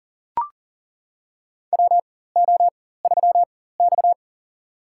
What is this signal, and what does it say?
Morse code, heard as a keyed tone, about 40 wpm, spaced as for 19 wpm, 700 Hz. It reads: WO3X
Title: USA Call Signs - 4 Characters - 4x Character Spacing - 40wpm